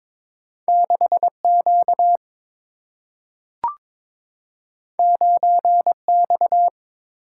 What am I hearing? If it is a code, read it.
Morse code, heard as a keyed tone, about 22 wpm, 700 Hz. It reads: 6Q 9X
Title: Single-Letter Single-Number 22wpm